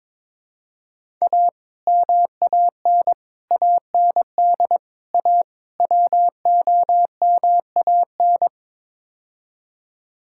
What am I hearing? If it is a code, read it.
Morse code, heard as a keyed tone, about 22 wpm, 700 Hz. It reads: A MAN AND A WOMAN
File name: practice / 22wpm / Build A Phrase - 5-Word Phrases - 22wpm